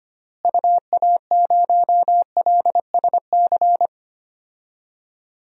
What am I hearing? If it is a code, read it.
Morse code, heard as a keyed tone, about 25 wpm, 700 Hz. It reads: UA0LSC
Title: DXCC-100 Call Signs - 6 Characters - Repeated 3x - 25wpm